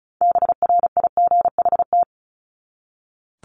Morse code, heard as a keyed tone, about 34 wpm, 700 Hz. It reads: BRIGHT
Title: Top 751-1000 Words 35wpm